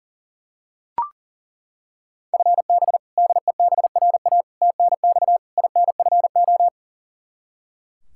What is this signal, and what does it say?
Morse code, heard as a keyed tone, about 40 wpm, 700 Hz. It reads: FB DEBRA TNX INFO
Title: Practice QSO - Exchange 3 - More Info - 40wpm